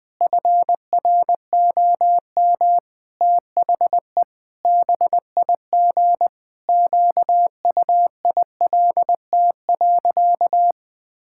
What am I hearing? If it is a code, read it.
Morse code, heard as a keyed tone, about 21 wpm, 700 Hz. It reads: FROM THE BIG QUILT.
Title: Pangram Challenge v2 - 20wpm